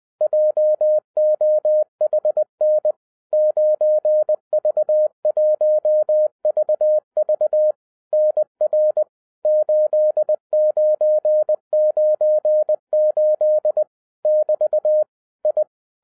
Morse code, heard as a keyed tone, about 20 wpm, 610 Hz. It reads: JOHN 9V1VV NR 8998 <BT> I